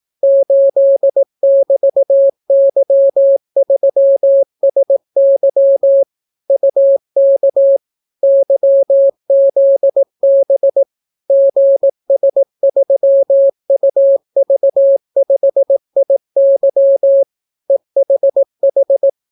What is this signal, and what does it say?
Morse code, heard as a keyed tone, about 18 wpm, 550 Hz. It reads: 8<BT>Y3SY UK YZB GS3UV5IY EHH